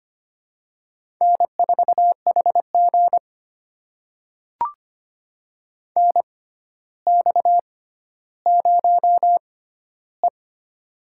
Morse code, heard as a keyed tone, about 25 wpm, spaced as for 13 wpm, 700 Hz. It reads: N4HG NX0E
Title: USA Call Signs - 4 Characters - 6x Character Spacing - 25wpm